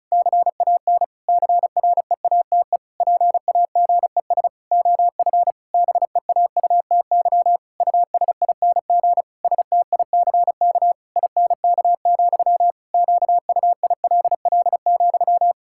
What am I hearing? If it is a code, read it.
Morse code, heard as a keyed tone, about 34 wpm, 700 Hz. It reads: CAN CREATE PAGES OF BEAUTY USING STICK INK, QUILL,
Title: Pangram Challenge v2 - 35wpm